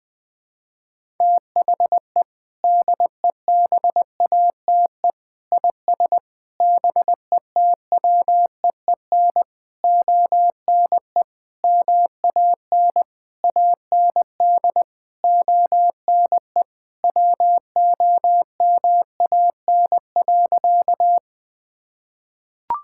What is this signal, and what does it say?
Morse code, heard as a keyed tone, about 21 wpm, 700 Hz. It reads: THE DEBATE IS BETWEEN ONE MAN AND ONE WOMAN.
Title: Sentences with 6-Word Phrases - 20wpm